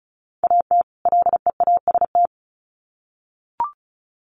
Morse code, heard as a keyed tone, about 34 wpm, 700 Hz. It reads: AT LEAST
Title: Sets of 2 Words 35wpm